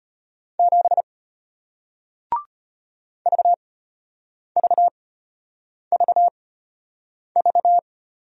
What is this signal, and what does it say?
Morse code, heard as a keyed tone, about 33 wpm, spaced as for 8 wpm, 700 Hz. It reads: 7 VVVV